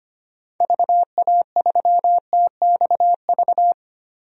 Morse code, heard as a keyed tone, about 25 wpm, 700 Hz. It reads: VA3TXV
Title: DXCC-100 Call Signs - 6 Characters - Repeated 2x - 25wpm